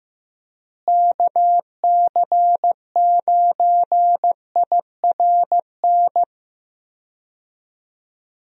Morse code, heard as a keyed tone, about 15 wpm, 700 Hz. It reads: KC9IRN
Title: USA Call Signs - 6 Characters - Repeated 3x - 15wpm